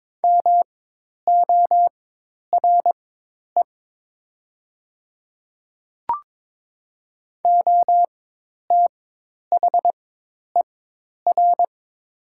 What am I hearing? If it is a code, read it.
Morse code, heard as a keyed tone, about 22 wpm, spaced as for 10 wpm, 700 Hz. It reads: MORE OTHER
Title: 4x Character Spacing - Top 100 Words - 22wpm